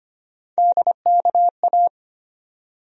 Morse code, heard as a keyed tone, about 25 wpm, 700 Hz. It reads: DKA
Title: USA Call Sign Postfixes - 3 Letters 25wpm